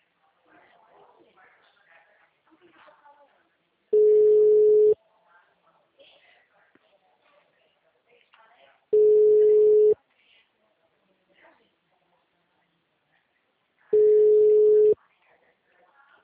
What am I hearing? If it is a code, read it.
Ringback tone (Europe, 425 Hz), with speech.
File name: out- -2010- - - .10387